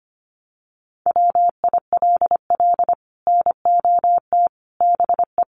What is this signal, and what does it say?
Morse code, heard as a keyed tone, about 25 wpm, 700 Hz. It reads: WILL NOT BE
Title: Sets of 5 Words 25wpm